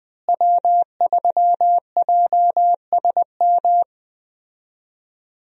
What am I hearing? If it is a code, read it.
Morse code, heard as a keyed tone, about 20 wpm, 700 Hz. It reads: W3JSM
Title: USA Call Signs - 5 Characters - Repeated 3x - 20wpm